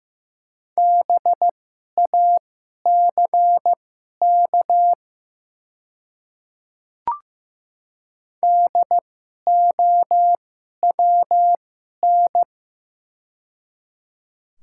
Morse code, heard as a keyed tone, about 15 wpm, spaced as for 11 wpm, 700 Hz. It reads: BACK DOWN
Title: 2x Character Spacing - Top 200 Words - 15wpm